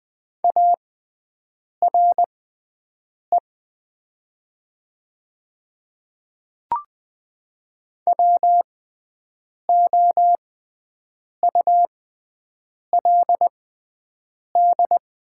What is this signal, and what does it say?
Morse code, heard as a keyed tone, about 20 wpm, spaced as for 7 wpm, 700 Hz. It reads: ARE WOULD
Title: 6x Character Spacing - Top 100 Words - 20wpm